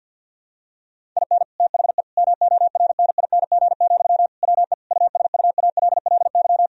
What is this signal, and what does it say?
Morse code, heard as a keyed tone, about 50 wpm, 700 Hz. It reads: IN THE MORNING, WE USUALLY
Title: Sentences from Top 1000 Words - Review Part 3 - 50wpm